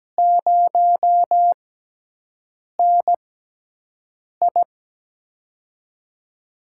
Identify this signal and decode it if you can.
Morse code, heard as a keyed tone, about 17 wpm, spaced as for 6 wpm, 700 Hz. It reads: 0NI
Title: USA Call Signs - 4 Characters - 6x Character Spacing - 17wpm